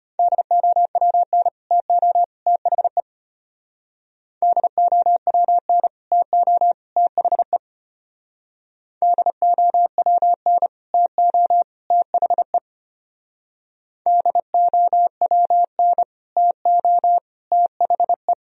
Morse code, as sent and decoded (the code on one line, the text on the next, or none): -.. --- .-- -. / - --- / - .... / -.. --- .-- -. / - --- / - .... / -.. --- .-- -. / - --- / - .... / -.. --- .-- -. / - --- / - .... .
DOWN TO THE DOWN TO THE DOWN TO THE DOWN TO THE